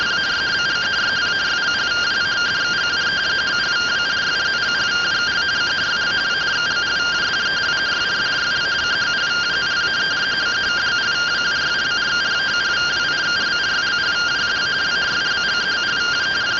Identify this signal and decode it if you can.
Two-tone teleprinter signal, about 170 Hz shift.